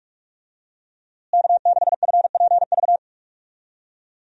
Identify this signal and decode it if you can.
Morse code, heard as a keyed tone, about 41 wpm, 700 Hz. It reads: K6FPV
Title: USA Call Signs - 5 Characters - Repeated 3x - 45wpm